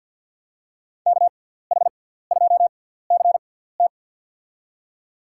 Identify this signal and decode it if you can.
Morse code, heard as a keyed tone, about 47 wpm, spaced as for 17 wpm, 690 Hz. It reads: KH2CT
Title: USA Call Signs - 6x Character Spacing - 50wpm